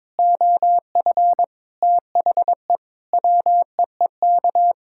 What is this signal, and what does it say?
Morse code, heard as a keyed tone, about 22 wpm, 700 Hz. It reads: OF THE WEEK
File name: practice / 22wpm / Sentences with 4-Word Phrases - 22wpm